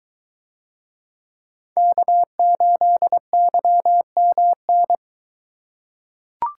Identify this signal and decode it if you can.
Morse code, heard as a keyed tone, about 23 wpm, 700 Hz. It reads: K8YMN